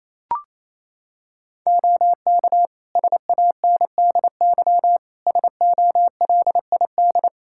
Morse code, heard as a keyed tone, about 28 wpm, 700 Hz. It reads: OK SANDY SOLID
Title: Practice QSO - Exchange 3 - More Info - 28wpm